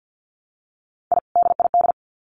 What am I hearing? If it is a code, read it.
Morse code, heard as a keyed tone, about 50 wpm, 700 Hz. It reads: I DID
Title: Sets of 2 Words 50wpm